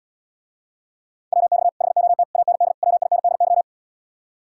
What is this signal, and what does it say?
Morse code, heard as a keyed tone, about 79 wpm, 690 Hz. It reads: WX HOT AND WINDY